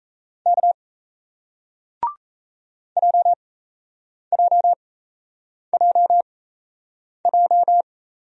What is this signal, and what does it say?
Morse code, heard as a keyed tone, about 38 wpm, spaced as for 8 wpm, 700 Hz. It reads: K JJJJ